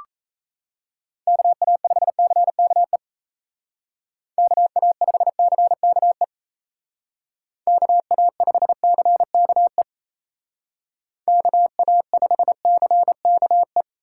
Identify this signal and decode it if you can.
Morse code, heard as a keyed tone, about 34 wpm, 700 Hz. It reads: KA5CKE KA5CKE KA5CKE KA5CKE
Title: Speed Racing - USA Call Signs 28wpm